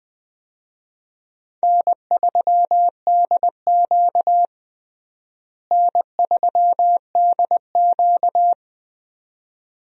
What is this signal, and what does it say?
Morse code, heard as a keyed tone, about 20 wpm, 700 Hz. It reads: N3DQ N3DQ
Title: USA Call Signs - 4 Characters - Repeated 2x - 20wpm